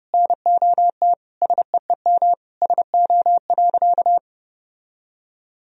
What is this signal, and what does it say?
Morse code, heard as a keyed tone, about 30 wpm, 700 Hz. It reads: NOT SEEM SO.
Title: Sentences from Top 1000 Words - Review Part 1 - 30wpm